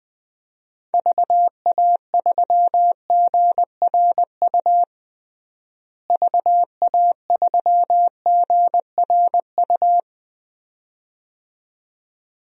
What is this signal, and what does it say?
Morse code, heard as a keyed tone, about 21 wpm, 700 Hz. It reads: VA3GRU VA3GRU
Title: DXCC-100 Call Signs - 6 Characters - Repeated 2x - 20wpm